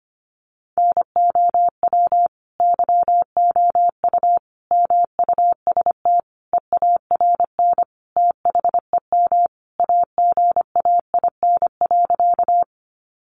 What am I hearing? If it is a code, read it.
Morse code, heard as a keyed tone, about 25 wpm, 700 Hz. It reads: NOW YOU MUST EARN THEM AGAIN.
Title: Sentences from Top 1500 Words 25wpm